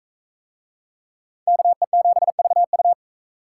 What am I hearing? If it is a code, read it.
Morse code, heard as a keyed tone, about 38 wpm, 700 Hz. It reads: KE7VU